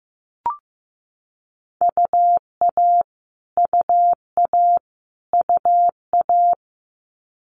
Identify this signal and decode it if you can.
Morse code, heard as a keyed tone, about 15 wpm, 700 Hz. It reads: UA UA UA
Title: Multi-letter Phonemes Repeated 3x 15wpm